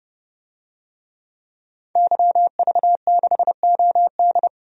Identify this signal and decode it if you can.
Morse code, heard as a keyed tone, about 30 wpm, 700 Hz. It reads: YV6OD
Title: DXCC-100 Call Signs - 5 Characters - 30wpm